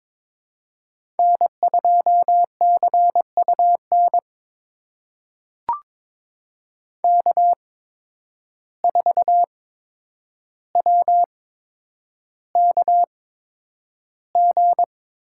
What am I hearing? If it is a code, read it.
Morse code, heard as a keyed tone, about 22 wpm, spaced as for 9 wpm, 700 Hz. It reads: N2CUN K4WKG
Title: USA Call Signs - 5 Characters - 8x Character Spacing - 22wpm